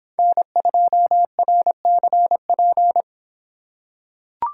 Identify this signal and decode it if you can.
Morse code, heard as a keyed tone, about 26 wpm, 700 Hz. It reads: N2RCP